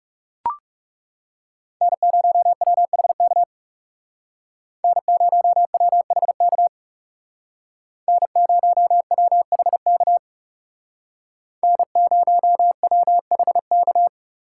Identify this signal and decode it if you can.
Morse code, heard as a keyed tone, about 36 wpm, 700 Hz. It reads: N0WHK N0WHK N0WHK N0WHK
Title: Speed Racing - USA 1x3 Call Signs 30wpm